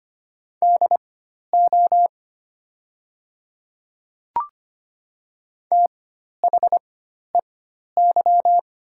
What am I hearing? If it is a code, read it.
Morse code, heard as a keyed tone, about 25 wpm, spaced as for 12 wpm, 690 Hz. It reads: DO THEY
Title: 4x Character Spacing - Top 500 Words - 25wpm